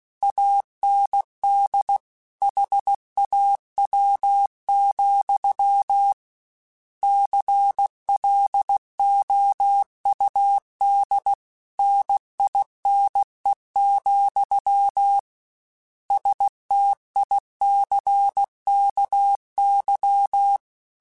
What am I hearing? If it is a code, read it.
Morse code, heard as a keyed tone, about 16 wpm, 800 Hz. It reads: AND HAW, CLOUD NINE, STICKY
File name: Misc / Misc-mp3s / Word Pairs @15